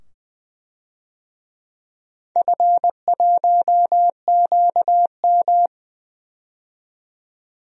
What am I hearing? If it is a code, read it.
Morse code, heard as a keyed tone, about 20 wpm, 700 Hz. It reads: F1QM
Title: DXCC-100 Call Signs - 4 Characters - 20wpm